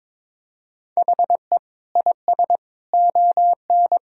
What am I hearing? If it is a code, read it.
Morse code, heard as a keyed tone, about 22 wpm, 700 Hz. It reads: HE IS ON
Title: Sentences from Top 1000 Words - Review Part 4 - 22wpm